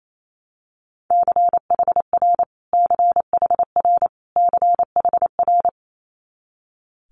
Morse code, heard as a keyed tone, about 28 wpm, 700 Hz. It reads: CHR CHR CHR
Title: Multi-letter Phonemes Repeated 3x 28wpm